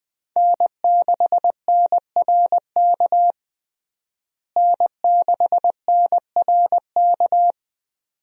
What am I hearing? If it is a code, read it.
Morse code, heard as a keyed tone, about 20 wpm, 700 Hz. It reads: N6NRK N6NRK
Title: USA Call Signs - 5 Characters - Repeated 2x - 20wpm